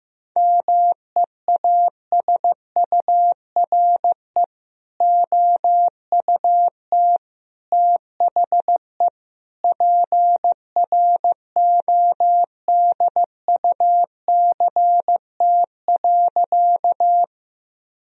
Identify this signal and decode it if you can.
Morse code, heard as a keyed tone, about 16 wpm, 700 Hz. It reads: MEASURE OUT THE PRODUCT.
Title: Sentences from Top 1000 Words - Review Part 2 - 15wpm